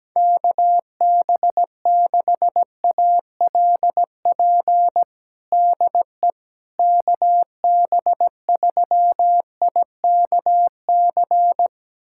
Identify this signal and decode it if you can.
Morse code, heard as a keyed tone, about 17 wpm, 700 Hz. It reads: KB6ALP DE KB3IKC
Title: USA Callsign DE Callsign 17wpm